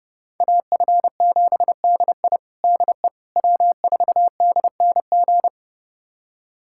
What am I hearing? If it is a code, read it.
Morse code, heard as a keyed tone, about 32 wpm, 700 Hz. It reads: AF7DI DE W4DNG
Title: USA Callsign DE Callsign - 5 Characters - 30wpm